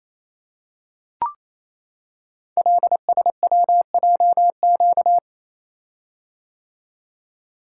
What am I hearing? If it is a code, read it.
Morse code, heard as a keyed tone, about 28 wpm, 700 Hz. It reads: LSWJQ
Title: Code Groups - 5 Letter:Numbers - 28wpm